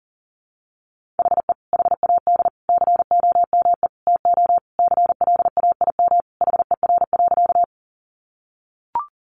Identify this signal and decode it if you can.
Morse code, heard as a keyed tone, about 40 wpm, 700 Hz. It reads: HE HAD COME TO CLAIM HER.
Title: Sentences from Top 1500 Words 40wpm